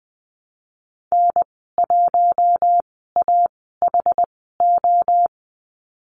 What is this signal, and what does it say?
Morse code, heard as a keyed tone, about 20 wpm, spaced as for 14 wpm, 700 Hz. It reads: N1AHO
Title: 076 Lesson 13g - Letter L - Call Signs 20wpm